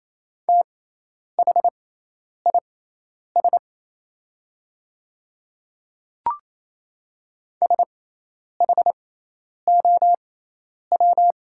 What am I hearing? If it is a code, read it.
Morse code, heard as a keyed tone, about 27 wpm, spaced as for 10 wpm, 700 Hz. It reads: THIS SHOW
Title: 6x Character Spacing - Top 200 Words - 28wpm